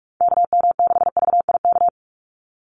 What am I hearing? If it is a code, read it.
Morse code, heard as a keyed tone, about 44 wpm, 700 Hz. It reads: KM6VIK
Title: USA Call Signs 45wpm